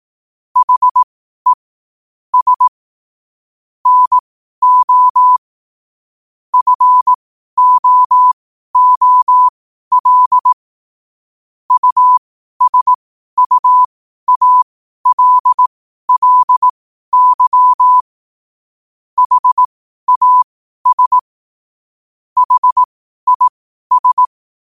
Morse code, heard as a keyed tone, about 18 wpm, spaced as for 12 wpm, 1000 Hz. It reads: HE S NO FOOL USUALLY HAS HIS